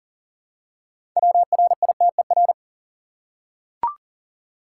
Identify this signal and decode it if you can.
Morse code, heard as a keyed tone, about 40 wpm, 700 Hz. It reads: WRITER
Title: Six Letter Words - 40wpm